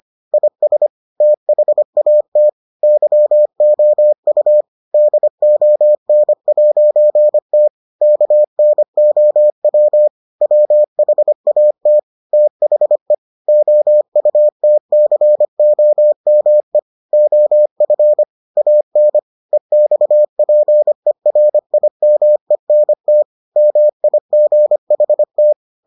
Morse code, heard as a keyed tone, about 25 wpm, 600 Hz. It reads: IS THAT YOU DON'T KNOW WHAT THE OUTCOME OF AN EXPERIMENT MIGHT